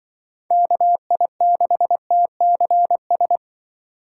Morse code, heard as a keyed tone, about 24 wpm, 700 Hz. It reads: KI6TCS
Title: Speed Racing - USA 2x3 Call Signs 20wpm